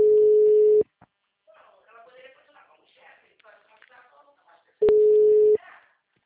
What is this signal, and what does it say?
Ringback tone (Europe, 425 Hz), with speech.
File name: out- -2013- - - .22